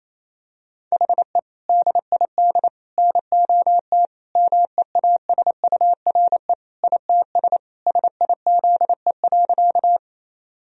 Morse code, heard as a keyed tone, about 28 wpm, 700 Hz. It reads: HE DID NOT MEASURE ITS SIZE.
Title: Sentences from Top 1000 Words - Review Part 2 - 28wpm